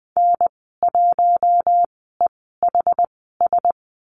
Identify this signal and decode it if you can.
Morse code, heard as a keyed tone, about 20 wpm, spaced as for 15 wpm, 700 Hz. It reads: N1EHS
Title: 193 Lesson 31g - Letter v - Call Signs 20wpm